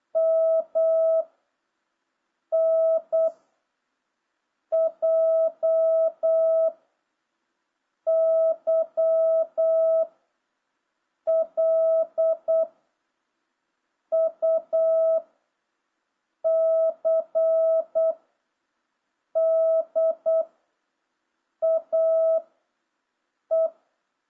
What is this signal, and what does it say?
Morse code, heard as a keyed tone, about 8 wpm, 650 Hz. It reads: M N J Y L U C D A E